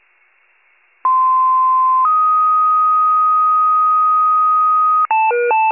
Two-tone page, heard as a one-second tone, then a three-second tone, about 1040 then 1250 Hz. It is heard from 1.0 to 5.1 s.